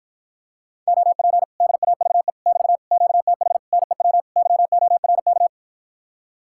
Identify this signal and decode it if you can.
Morse code, heard as a keyed tone, about 60 wpm, 700 Hz. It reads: OP DAVE <BT> QTH NEW YORK